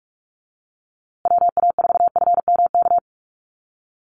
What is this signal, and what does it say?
Morse code, heard as a keyed tone, about 44 wpm, 690 Hz. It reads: WA4FMK
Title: USA Call Signs Repeated 3x 45wpm